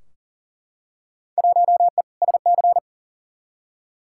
Morse code, heard as a keyed tone, about 40 wpm, 700 Hz. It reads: 1E SC
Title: Field Day Exchanges - 40wpm